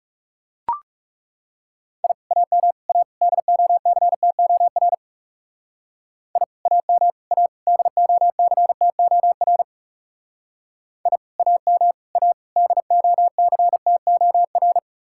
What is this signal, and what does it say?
Morse code, heard as a keyed tone, about 35 wpm, 700 Hz. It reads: I AM A DOCTOR I AM A DOCTOR I AM A DOCTOR
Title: Speed-Racing Common QSO Phrases Encore - 30wpm